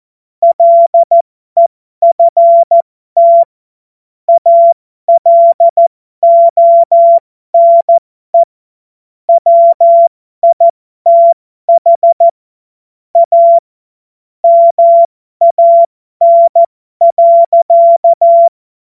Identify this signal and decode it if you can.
Morse code, heard as a keyed tone, about 14 wpm, spaced as for 12 wpm, 680 Hz. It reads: LEFT ALONE WITH A MAN.